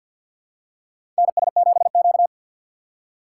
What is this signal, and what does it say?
Morse code, heard as a keyed tone, about 52 wpm, 700 Hz. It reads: NS7Q
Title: USA Call Signs - 4 Characters - 6x Character Spacing - 50wpm